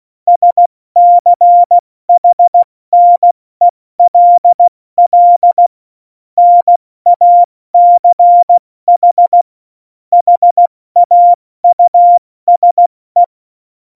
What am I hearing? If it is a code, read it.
Morse code, heard as a keyed tone, about 16 wpm, 700 Hz. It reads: SCHNELL NACH HAUSE
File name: Morsekurs / Witze / Witz 01. Oktober 2019, 70 F80